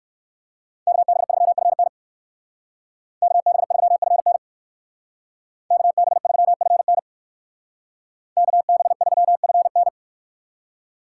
Morse code, heard as a keyed tone, about 59 wpm, 700 Hz. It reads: KB3FN KB3FN KB3FN KB3FN